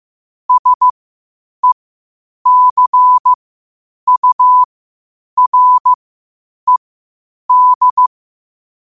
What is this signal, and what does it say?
Morse code, heard as a keyed tone, about 15 wpm, spaced as for 8 wpm, 1000 Hz. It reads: SECURED
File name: qotdfiles / 8wpm